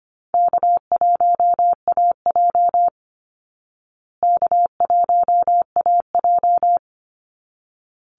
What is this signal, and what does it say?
Morse code, heard as a keyed tone, about 25 wpm, 700 Hz. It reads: K1AJ K1AJ
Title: Call Signs Repeated 3x - 25wpm